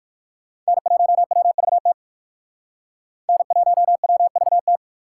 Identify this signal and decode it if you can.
Morse code, heard as a keyed tone, about 69 wpm, spaced as for 58 wpm, 700 Hz. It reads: N1WVT N1WVT